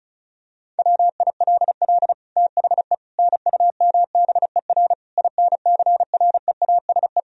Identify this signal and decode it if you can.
Morse code, heard as a keyed tone, about 36 wpm, 700 Hz. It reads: WILL THE NUMBER INCREASE
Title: Sentences from Top 1000 Words - Review Part 4 - 35wpm